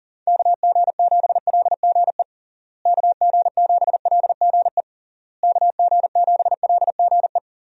Morse code, heard as a keyed tone, about 40 wpm, 700 Hz. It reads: KG7LGE KG7LGE KG7LGE
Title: USA Call Signs - 6 Characters - Repeated 3x - 40wpm